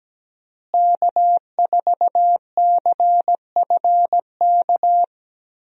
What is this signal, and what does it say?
Morse code, heard as a keyed tone, about 17 wpm, 700 Hz. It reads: K4CFK
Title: USA Call Signs - 5 Characters - Repeated 3x - 17wpm